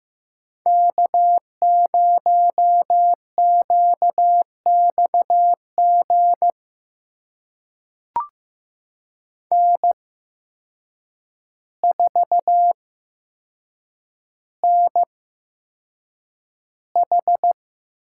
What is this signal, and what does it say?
Morse code, heard as a keyed tone, about 15 wpm, spaced as for 7 wpm, 700 Hz. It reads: K0QXG N4NH
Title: USA Call Signs - 8x Character Spacing - 15wpm